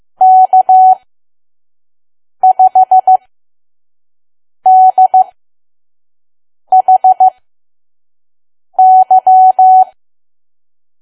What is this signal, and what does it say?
Morse code, heard as a keyed tone, about 15 wpm, spaced as for 5 wpm, 750 Hz. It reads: K5DHY
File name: Archive / 5 WPM files / PM